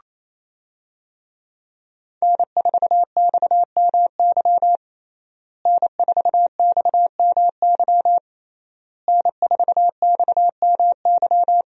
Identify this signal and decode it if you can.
Morse code, heard as a keyed tone, about 27 wpm, 700 Hz. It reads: N4XMY N4XMY N4XMY
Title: USA Call Signs - 5 Characters - Repeated 3x - 28wpm